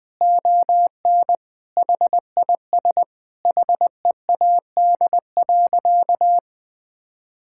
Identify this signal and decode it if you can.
Morse code, heard as a keyed tone, about 20 wpm, 700 Hz. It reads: ON HIS HEAD.
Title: Sentences from Top 1600 Words 20wpm